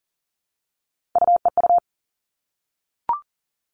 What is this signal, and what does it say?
Morse code, heard as a keyed tone, about 40 wpm, 700 Hz. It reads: UEU
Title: All Cut Numbers - 3-Digit 40wpm